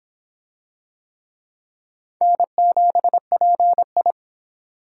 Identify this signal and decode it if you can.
Morse code, heard as a keyed tone, about 23 wpm, 700 Hz. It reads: N7PI